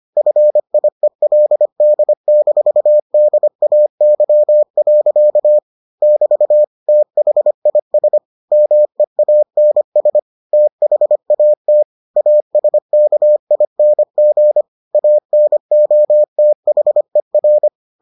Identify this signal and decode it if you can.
Morse code, heard as a keyed tone, about 25 wpm, 600 Hz. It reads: FIELD-DAY. <BT> THIS MEANS THAT ASKING ANOTHER